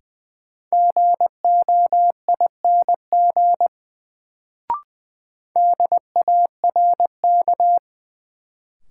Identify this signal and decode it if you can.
Morse code, heard as a keyed tone, about 21 wpm, 700 Hz. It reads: GOING DARK
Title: Sentences from Top 500 Words - New Word Drill - 20wpm